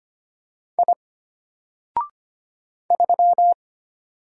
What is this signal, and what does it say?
Morse code, heard as a keyed tone, about 25 wpm, 700 Hz. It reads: I 3
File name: Single-Letter-Number-25wpm